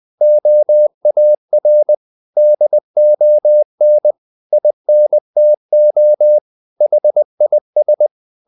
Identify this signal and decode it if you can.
Morse code, heard as a keyed tone, about 20 wpm, 600 Hz. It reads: OAR DON INTO HIS
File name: the old man and the sea part 2 20wpm